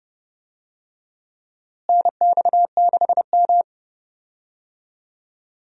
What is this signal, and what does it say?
Morse code, heard as a keyed tone, about 30 wpm, 700 Hz. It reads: NX6M
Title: USA Call Signs - 4 Characters - Repeated 2x - 30wpm